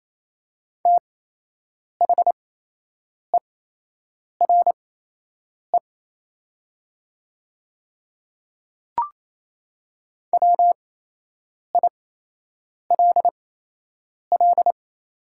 Morse code, heard as a keyed tone, about 28 wpm, spaced as for 8 wpm, 700 Hz. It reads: THERE WILL
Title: 8x Character Spacing - Top 200 Words - 28wpm